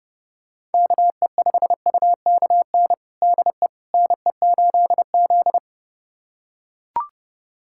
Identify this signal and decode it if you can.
Morse code, heard as a keyed tone, about 29 wpm, 700 Hz. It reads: KE5UKN DE NE8Z
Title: USA Callsign DE Callsign 30wpm